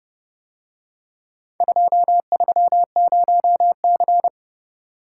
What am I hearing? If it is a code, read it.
Morse code, heard as a keyed tone, about 28 wpm, 700 Hz. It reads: 230C